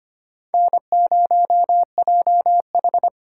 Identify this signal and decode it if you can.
Morse code, heard as a keyed tone, about 32 wpm, 700 Hz. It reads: N0JH